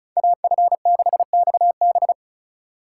Morse code, heard as a keyed tone, about 34 wpm, 700 Hz. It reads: AF6XB
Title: USA Call Signs - 5 Characters - 4x Character Spacing - 35wpm